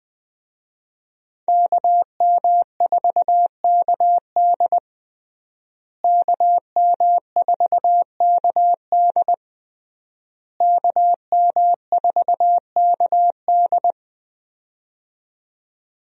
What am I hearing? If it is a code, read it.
Morse code, heard as a keyed tone, about 20 wpm, 700 Hz. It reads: KM4KD KM4KD KM4KD
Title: USA Call Signs - 5 Characters - Repeated 3x - 20wpm